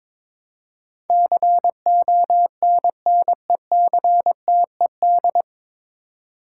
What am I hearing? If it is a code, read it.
Morse code, heard as a keyed tone, about 22 wpm, 700 Hz. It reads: CONNECTED
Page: Morse Code Ninja practice 22wpm